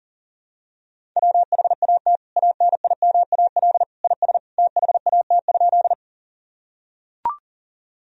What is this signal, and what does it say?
Morse code, heard as a keyed tone, about 37 wpm, 700 Hz. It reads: WHAT ANIMAL IS THAT?
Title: Sentences from Top 1000 Words - Review Part 1 - 40wpm